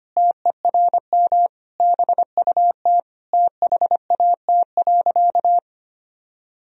Morse code, heard as a keyed tone, about 25 wpm, 700 Hz. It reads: TERM BUT THAT.
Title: Sentences from Top 1000 Words - Review Part 4 - 25wpm